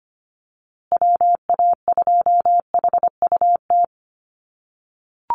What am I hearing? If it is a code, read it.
Morse code, heard as a keyed tone, about 25 wpm, 700 Hz. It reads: WA2HUT
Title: USA Call Signs Repeated 2x 25wpm